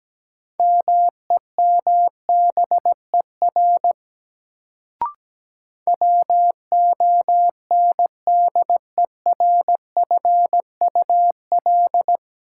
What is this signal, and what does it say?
Morse code, heard as a keyed tone, about 17 wpm, 700 Hz. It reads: MEMBER WONDERFUL
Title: Sentences from Top 1100 Words - New Word Drill Alt - 17wpm